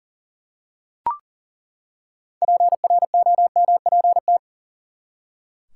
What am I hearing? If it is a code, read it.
Morse code, heard as a keyed tone, about 38 wpm, 700 Hz. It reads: PROMPT
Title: Six Letter Words - 40wpm